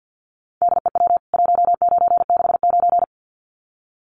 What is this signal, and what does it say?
Morse code, heard as a keyed tone, about 49 wpm, 700 Hz. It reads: DEW 1969
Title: CWT Exchange 50wpm